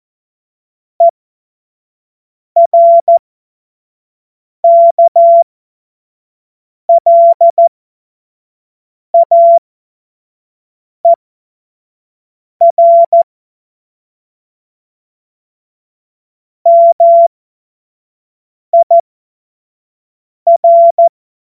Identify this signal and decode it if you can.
Morse code, heard as a keyed tone, about 14 wpm, spaced as for 5 wpm, 680 Hz. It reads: ERKLAER MIR
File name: Morsekurs / Witze / Witz 10. Mai 2022 25 F70